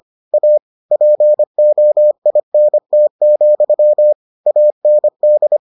Morse code, heard as a keyed tone, about 25 wpm, 600 Hz. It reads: A POINT, AND